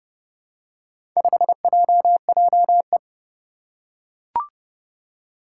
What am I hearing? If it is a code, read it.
Morse code, heard as a keyed tone, about 30 wpm, 700 Hz. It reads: 5JJE
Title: Four Challenging Letter-Number Combinations 30wpm